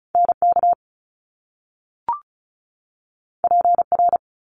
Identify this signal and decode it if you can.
Morse code, heard as a keyed tone, about 36 wpm, 700 Hz. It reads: NK PR
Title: Top Two Letter Combinations in Words 35wpm